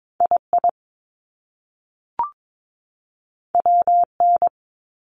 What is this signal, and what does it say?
Morse code, heard as a keyed tone, about 22 wpm, 700 Hz. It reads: II WN
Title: Top Two Letter Combinations in Words 22wpm